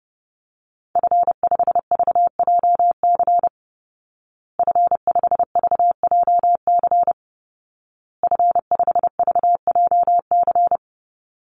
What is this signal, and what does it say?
Morse code, heard as a keyed tone, about 30 wpm, 700 Hz. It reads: F5VJC F5VJC F5VJC
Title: Call Signs Repeated 2x - 30wpm